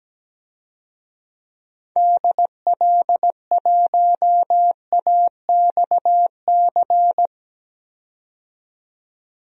Morse code, heard as a keyed tone, about 17 wpm, 700 Hz. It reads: DL1AXC
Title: DXCC-100 Call Signs - 6 Characters - 17wpm